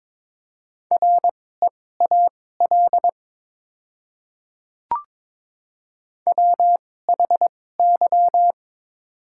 Morse code, heard as a keyed tone, about 23 wpm, spaced as for 16 wpm, 700 Hz. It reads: REAL WHY
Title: 2x Character Spacing - Top 500 Words - 22wpm